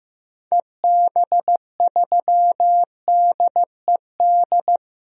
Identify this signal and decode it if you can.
Morse code, heard as a keyed tone, about 15 wpm, 700 Hz. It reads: EB3DED
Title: DXCC-100 Call Signs - 6 Characters - 15wpm